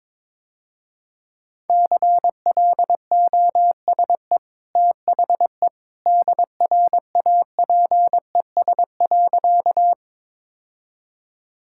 Morse code, heard as a keyed tone, about 22 wpm, 700 Hz. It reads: CLOSE THE DRAPES.
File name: practice / 22wpm / Easy Sentences with Hints - 22wpm